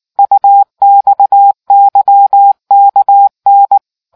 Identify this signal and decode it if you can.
Morse code, heard as a keyed tone, about 19 wpm, 800 Hz. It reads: UXYKN